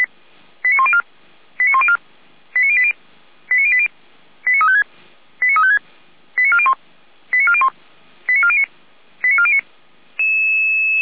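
Five-tone selective call (ZVEI), heard as quick runs of five tones, tones of about 70 ms. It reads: 89104 89104 89080 89080 89356 89356 89401 89401 89409 89409
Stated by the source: ZVEI five-tone call received by SDR.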